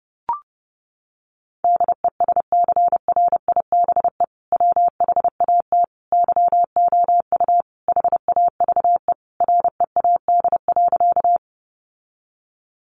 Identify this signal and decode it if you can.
Morse code, heard as a keyed tone, about 30 wpm, 700 Hz. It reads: DESCRIBE WHAT YOU HAVE READ.
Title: Sentences from Top 1000 Words 30wpm